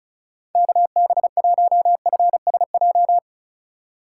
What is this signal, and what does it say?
Morse code, heard as a keyed tone, about 36 wpm, 700 Hz. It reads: KB1FSJ
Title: USA Call Signs - 6 Characters - 4x Character Spacing - 35wpm